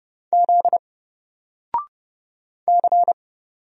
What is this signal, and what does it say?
Morse code, heard as a keyed tone, about 30 wpm, 700 Hz. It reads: Z C
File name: Single-Letter-Number-30wpm